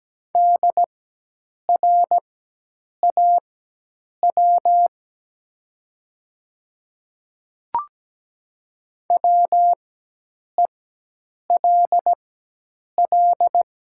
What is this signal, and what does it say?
Morse code, heard as a keyed tone, about 17 wpm, spaced as for 8 wpm, 700 Hz. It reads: DRAW WELL
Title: 4x Character Spacing - Top 500 Words - 17wpm